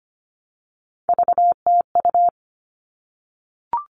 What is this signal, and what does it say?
Morse code, heard as a keyed tone, about 25 wpm, 700 Hz. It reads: VTU
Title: All Cut Numbers - 3-Digit 25wpm